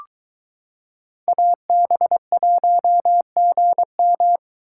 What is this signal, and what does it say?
Morse code, heard as a keyed tone, about 23 wpm, 700 Hz. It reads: AB1GM